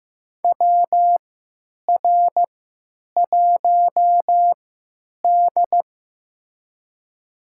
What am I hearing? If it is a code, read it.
Morse code, heard as a keyed tone, about 15 wpm, spaced as for 9 wpm, 700 Hz. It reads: WR1D
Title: USA Call Signs - 4 Characters - 3x Character Spacing - 15wpm